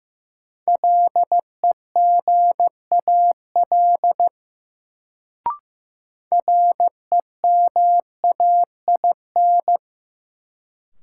Morse code, heard as a keyed tone, about 15 wpm, 700 Hz. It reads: LEGAL REMAIN
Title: Sentences from Top 1200 Words - New Word Drill - 15wpm